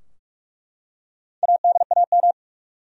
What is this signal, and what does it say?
Morse code, heard as a keyed tone, about 46 wpm, 700 Hz. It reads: ADAM
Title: USA Common Names - Max 4 Letters - 45wpm